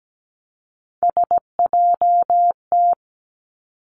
Morse code, heard as a keyed tone, about 17 wpm, 700 Hz. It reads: SJT
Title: Three Challenging Letter-Number Combinations 17wpm